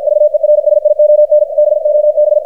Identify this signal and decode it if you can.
Morse code, heard as a keyed tone, about 52 wpm, 600 Hz, with radio noise. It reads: VERSION BOB
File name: archived / CRSnet